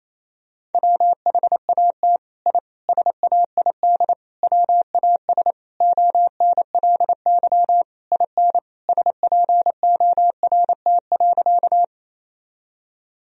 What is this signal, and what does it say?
Morse code, heard as a keyed tone, about 28 wpm, 700 Hz. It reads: WHAT I SAID WAS ONLY IN SPORT.
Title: Sentences from Top 1000 Words - Review Part 4 - 28wpm